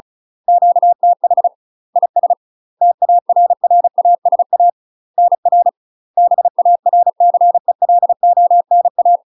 Morse code, heard as a keyed tone, about 36 wpm, 700 Hz. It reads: QTH IS TARRASA NR BARCELONA
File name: UK MASTER QSOs TRAINING/35WPM